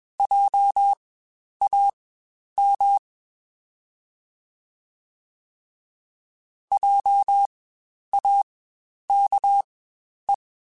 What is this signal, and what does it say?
Morse code, heard as a keyed tone, about 21 wpm, spaced as for 10 wpm, 800 Hz. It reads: JAM JAKE
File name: Spaced Alphas / Spaced Alphas mp3s / Js sp @20